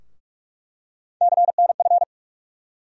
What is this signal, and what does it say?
Morse code, heard as a keyed tone, about 43 wpm, 700 Hz. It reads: CNF
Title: USA Call Sign Postfixes - 3 Letters 45wpm